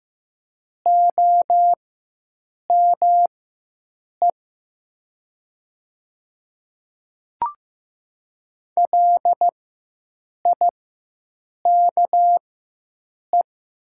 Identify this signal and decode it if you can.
Morse code, heard as a keyed tone, about 15 wpm, spaced as for 7 wpm, 700 Hz. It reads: OME LIKE
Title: 4x Character Spacing - Top 300 Words - 15wpm